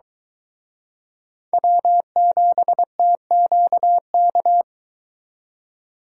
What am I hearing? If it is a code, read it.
Morse code, heard as a keyed tone, about 22 wpm, 700 Hz. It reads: W7TQK